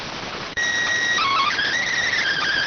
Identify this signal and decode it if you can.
SSTV (slow-scan television), PD 290.